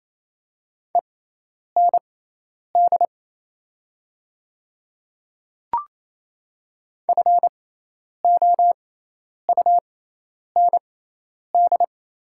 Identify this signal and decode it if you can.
Morse code, heard as a keyed tone, about 28 wpm, spaced as for 10 wpm, 700 Hz. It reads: END FOUND
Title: 6x Character Spacing - Top 300 Words - 28wpm